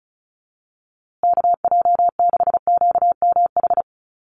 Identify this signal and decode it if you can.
Morse code, heard as a keyed tone, about 34 wpm, 700 Hz. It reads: KJ6QMH
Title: USA Call Signs Repeated 2x 35wpm